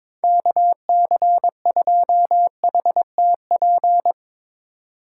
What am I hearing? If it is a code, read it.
Morse code, heard as a keyed tone, about 22 wpm, 700 Hz. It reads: KC2HTP
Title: USA Call Signs - 6 Characters - 2x Character Spacing - 22wpm